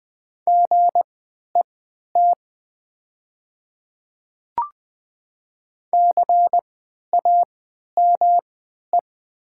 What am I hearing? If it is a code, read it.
Morse code, heard as a keyed tone, about 20 wpm, spaced as for 11 wpm, 700 Hz. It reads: GET CAME
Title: 3x Character Spacing - Top 200 Words - 20wpm